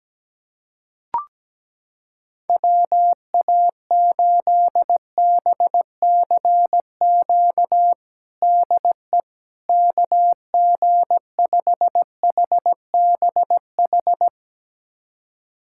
Morse code, heard as a keyed tone, about 17 wpm, 700 Hz. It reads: WA8BCQ DE KG5HBH
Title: USA Callsign DE Callsign - 6 Characters - 17wpm